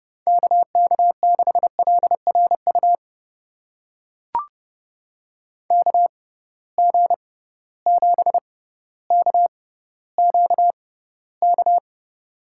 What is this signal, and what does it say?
Morse code, heard as a keyed tone, about 32 wpm, spaced as for 16 wpm, 700 Hz. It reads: KK6LRU KG7KQK
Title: USA Call Signs - 6 Characters - 6x Character Spacing - 30wpm